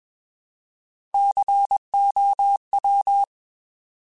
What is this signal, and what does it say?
Morse code, heard as a keyed tone, about 21 wpm, 800 Hz. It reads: COW